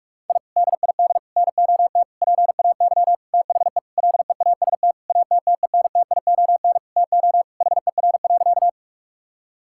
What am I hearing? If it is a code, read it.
Morse code, heard as a keyed tone, about 46 wpm, 700 Hz. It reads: I DID NOT PAY THE LEAST ATTENTION TO HER.
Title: Sentences from Top 1000 Words - Review Part 6 - 45wpm